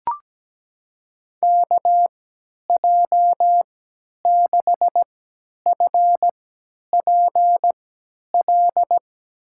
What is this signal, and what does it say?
Morse code, heard as a keyed tone, about 17 wpm, spaced as for 10 wpm, 700 Hz. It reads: KJ6FPL
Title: USA Call Signs - 3x Character Spacing - 17wpm